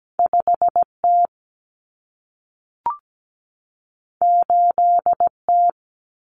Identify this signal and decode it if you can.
Morse code, heard as a keyed tone, about 17 wpm, 700 Hz. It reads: 5T 8T
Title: Common Cut Numbers - 2-Digit 17wpm